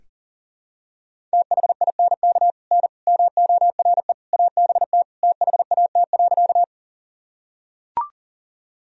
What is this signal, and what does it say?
Morse code, heard as a keyed tone, about 37 wpm, 700 Hz. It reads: THINK N MORE ABT THAT.
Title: Sentences from Top 200 Words and Abbreviations 40wpm